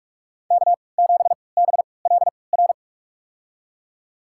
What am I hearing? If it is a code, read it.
Morse code, heard as a keyed tone, about 47 wpm, spaced as for 26 wpm, 700 Hz. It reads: K7BLR
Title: USA Call Signs - 5 Characters - 3x Character Spacing - 45wpm